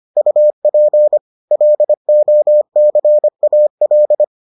Morse code, heard as a keyed tone, about 25 wpm, 600 Hz. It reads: UP LOCAL